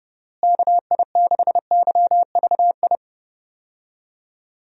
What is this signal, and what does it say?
Morse code, heard as a keyed tone, about 30 wpm, 700 Hz. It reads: KI6YVI
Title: USA Call Signs - 6 Characters - Repeated 2x - 30wpm